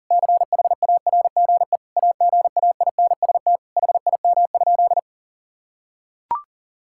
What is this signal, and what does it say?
Morse code, heard as a keyed tone, about 40 wpm, 700 Hz. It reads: CHARGE AGAINST HIM?
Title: Sentences from Top 1000 Words - Review Part 6 - 40wpm